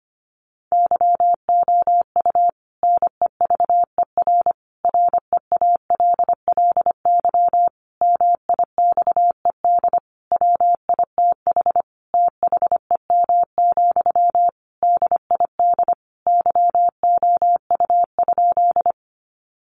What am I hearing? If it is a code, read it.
Morse code, heard as a keyed tone, about 25 wpm, 700 Hz. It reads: YOU NEVER REALLY MIXED WITH THEM, DID YOU?
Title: Sentences from Top 1500 Words 25wpm